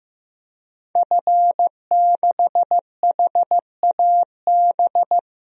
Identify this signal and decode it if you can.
Morse code, heard as a keyed tone, about 15 wpm, 700 Hz. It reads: F6HAB
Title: DXCC-100 Call Signs - 5 Characters - Repeated 3x - 15wpm